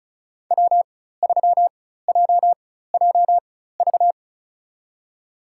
Morse code, heard as a keyed tone, about 36 wpm, spaced as for 16 wpm, 700 Hz. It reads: W3JJV
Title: USA Call Signs - 5 Characters - 4x Character Spacing - 35wpm